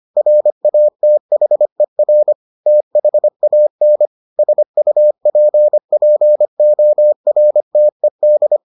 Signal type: Morse code

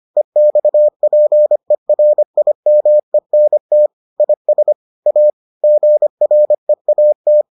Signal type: Morse code

